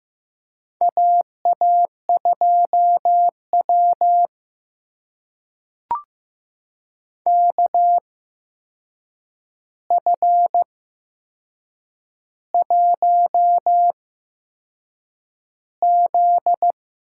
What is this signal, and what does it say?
Morse code, heard as a keyed tone, about 15 wpm, spaced as for 6 wpm, 700 Hz. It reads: AA2W KF1Z
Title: USA Call Signs - 4 Characters - 8x Character Spacing - 15wpm